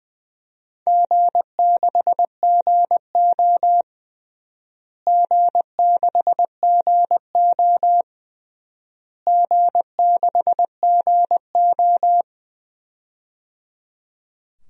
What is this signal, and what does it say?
Morse code, heard as a keyed tone, about 21 wpm, 700 Hz. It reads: G6GO G6GO G6GO
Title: DXCC-100 Call Signs - 4 Characters - Repeated 3x - 20wpm